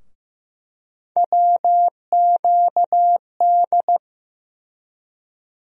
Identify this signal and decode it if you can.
Morse code, heard as a keyed tone, about 15 wpm, 700 Hz. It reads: WQD
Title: USA Call Sign Postfixes - 3 Letters 15wpm